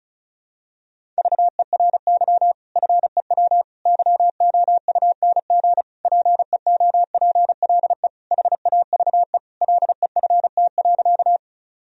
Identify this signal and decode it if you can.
Morse code, heard as a keyed tone, about 34 wpm, 700 Hz. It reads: VERY FEW YOUNG PEOPLE HAVE LEFT.
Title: Sentences from Top 1000 Words - Review Part 1 - 35wpm